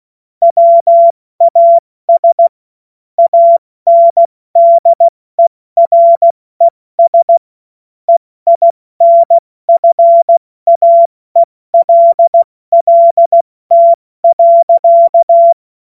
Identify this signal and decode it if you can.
Morse code, heard as a keyed tone, about 16 wpm, 680 Hz. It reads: WAS ANDERES EINFAELLT.
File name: Morsekurs / Witze / Witz 01. November 2022 70 F80